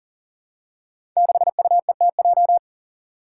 Morse code, heard as a keyed tone, about 37 wpm, 700 Hz. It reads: 6UETJ
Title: Five Challenging Letter-Number Combinations 40wpm